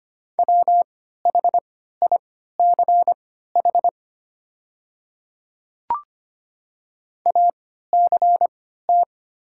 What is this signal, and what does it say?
Morse code, heard as a keyed tone, about 25 wpm, spaced as for 14 wpm, 690 Hz. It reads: WHICH ACT
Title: 3x Character Spacing - Top 200 Words - 25wpm